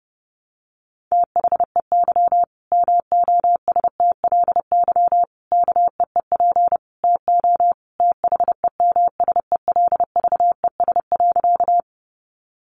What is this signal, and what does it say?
Morse code, heard as a keyed tone, about 30 wpm, 700 Hz. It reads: THEY MOSTLY KEEP TO THEMSELVES.
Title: Sentences from Top 1300 Words 30wpm